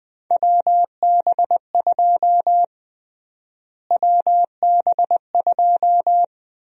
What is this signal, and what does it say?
Morse code, heard as a keyed tone, about 20 wpm, 700 Hz. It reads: WB2 WB2
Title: USA Call Sign Prefixes - Repeated 2x - 20wpm